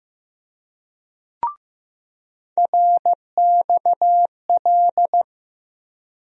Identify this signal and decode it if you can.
Morse code, heard as a keyed tone, about 15 wpm, 700 Hz. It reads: RXL
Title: USA Call Sign Postfixes - 3 Letters 15wpm